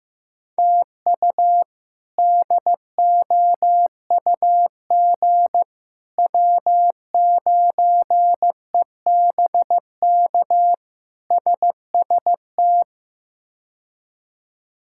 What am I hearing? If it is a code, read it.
Morse code, heard as a keyed tone, about 15 wpm, 700 Hz. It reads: TU DOUG W9EBK SST
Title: K1USN SST QSO Practice - 15wpm